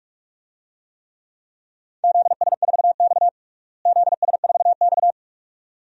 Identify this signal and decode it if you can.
Morse code, heard as a keyed tone, about 41 wpm, 700 Hz. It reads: ZS4X ZS4X
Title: DXCC-100 Call Signs - 4 Characters - Repeated 2x - 45wpm